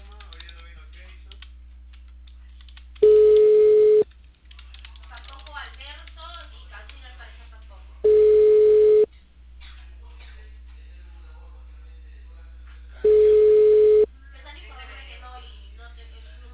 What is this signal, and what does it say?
Ringback tone (Europe, 425 Hz), with speech.